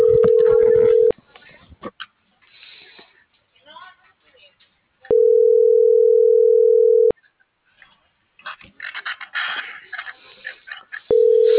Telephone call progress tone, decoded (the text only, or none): ringback tone